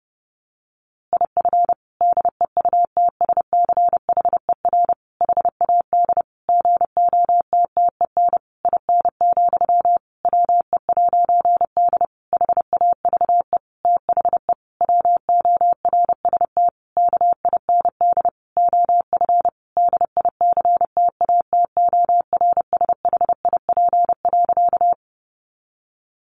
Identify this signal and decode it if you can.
Morse code, heard as a keyed tone, about 28 wpm, 700 Hz. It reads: IF DEUTSCHER HAD GOTTEN IN, WE'D HAVE THE WORST KIND OF DICTATORSHIP.
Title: A Sound Of Thunder 30wpm